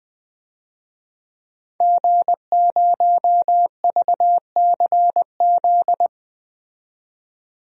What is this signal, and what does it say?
Morse code, heard as a keyed tone, about 20 wpm, 700 Hz. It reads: G0VCZ
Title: DXCC-100 Call Signs - 5 Characters - 20wpm